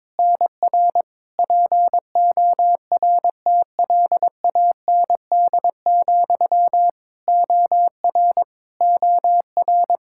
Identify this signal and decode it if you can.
Morse code, heard as a keyed tone, about 22 wpm, 700 Hz. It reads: NR PORTLAND, OR OR
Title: Practice QSO - Exchange 1 - Beginning - 22wpm